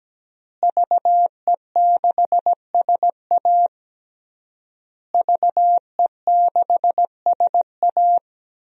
Morse code, heard as a keyed tone, about 17 wpm, 700 Hz. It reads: VE6SA VE6SA
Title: DXCC-100 Call Signs - 5 Characters - Repeated 3x - 17wpm